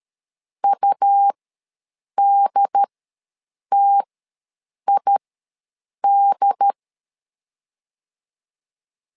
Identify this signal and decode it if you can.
Morse code, heard as a keyed tone, about 13 wpm, spaced as for 7 wpm, 800 Hz. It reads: UDTID
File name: LEZIONI / 2 UHDN / 2 UHD ws 12wpm 1722